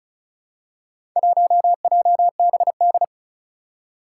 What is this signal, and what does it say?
Morse code, heard as a keyed tone, about 36 wpm, 700 Hz. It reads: 1JBD
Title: Four Challenging Letter-Number Combinations 35wpm